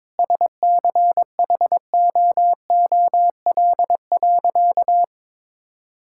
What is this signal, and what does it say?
Morse code, heard as a keyed tone, about 22 wpm, 700 Hz. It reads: SCHOOL.
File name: practice / 22wpm / Sentences with 2-Word Phrases - 22wpm